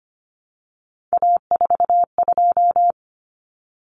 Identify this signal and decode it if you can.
Morse code, heard as a keyed tone, about 25 wpm, 700 Hz. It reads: A42
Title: Common Cut Numbers - 3-Digit 25wpm